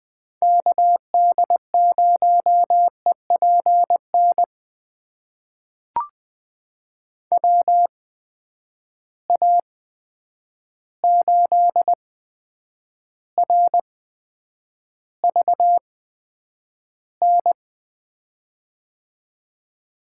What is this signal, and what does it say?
Morse code, heard as a keyed tone, about 20 wpm, spaced as for 9 wpm, 700 Hz. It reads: KD0EPN WA8RVN
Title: USA Call Signs - 6 Characters - 8x Character Spacing - 20wpm